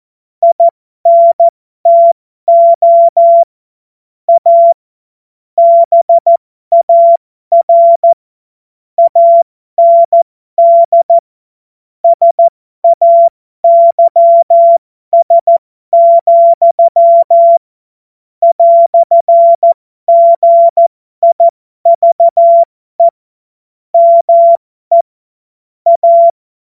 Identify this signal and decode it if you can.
Morse code, heard as a keyed tone, about 14 wpm, spaced as for 12 wpm, 680 Hz. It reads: INTO A BAR AND SAYS, "GIVE ME A
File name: Morsekurs / Witze